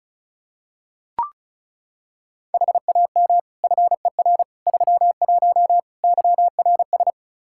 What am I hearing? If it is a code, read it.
Morse code, heard as a keyed tone, about 36 wpm, 700 Hz. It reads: HAM FER 31 YRS
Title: K1USN SST QSO Practice 35wpm